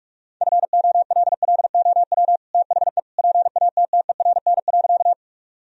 Morse code, heard as a keyed tone, about 41 wpm, 700 Hz. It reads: FOLLOW THE PATTERN.
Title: Sentences from Top 1000 Words - Review Part 2 - 45wpm